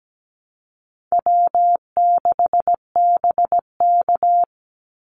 Morse code, heard as a keyed tone, about 17 wpm, 700 Hz. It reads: W6BK
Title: Call Signs Repeated 3x - 17wpm